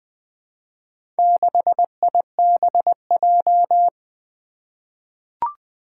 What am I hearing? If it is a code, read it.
Morse code, heard as a keyed tone, about 20 wpm, 700 Hz. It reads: 6IBJ
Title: Four Challenging Letter-Number Combinations 20wpm